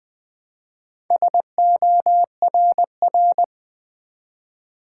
Morse code, heard as a keyed tone, about 20 wpm, 700 Hz. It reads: SORR
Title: Build A Word - 1 Letter At A Time - 5-Letter Words - 20wpm